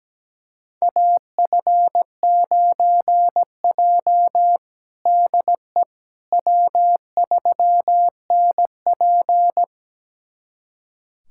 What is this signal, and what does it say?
Morse code, heard as a keyed tone, about 17 wpm, 700 Hz. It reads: AF9J DE W3NP
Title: USA Callsign DE Callsign - 4 Characters - 17wpm